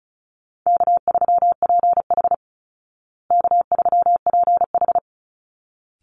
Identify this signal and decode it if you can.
Morse code, heard as a keyed tone, about 34 wpm, 700 Hz. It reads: K3PH K3PH
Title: Call Signs Repeated 3x - 35wpm